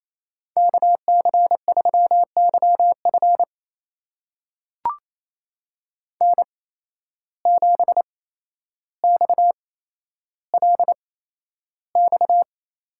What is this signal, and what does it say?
Morse code, heard as a keyed tone, about 28 wpm, spaced as for 12 wpm, 700 Hz. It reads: KC3YF N7XLX
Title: USA Call Signs - 5 Characters - 8x Character Spacing - 28wpm